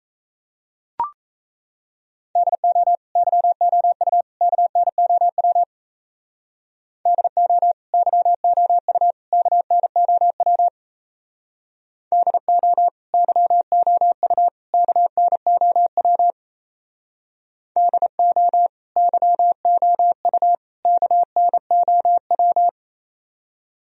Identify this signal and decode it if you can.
Morse code, heard as a keyed tone, about 34 wpm, 700 Hz. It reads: DO YOU KNOW DO YOU KNOW DO YOU KNOW DO YOU KNOW
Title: Speed-Racing - Sets of 3 Words 28wpm